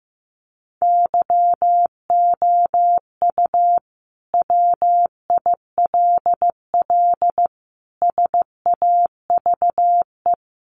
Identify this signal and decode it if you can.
Morse code, heard as a keyed tone, about 15 wpm, 700 Hz. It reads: YOU WILL SAVE
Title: Sentences from Top 1100 Words 15wpm